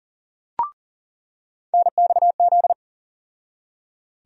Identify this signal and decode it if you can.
Morse code, heard as a keyed tone, about 38 wpm, 700 Hz. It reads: NXZ
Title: USA Call Sign Postfixes - 3 Letters 40wpm